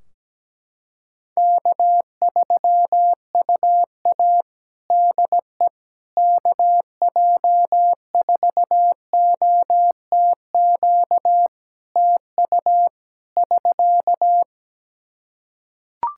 Morse code, heard as a keyed tone, about 17 wpm, 700 Hz. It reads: K3UA DE KJ4OTQ TU <SK>
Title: Practice QSO - Exchange 4 - Ending - 17wpm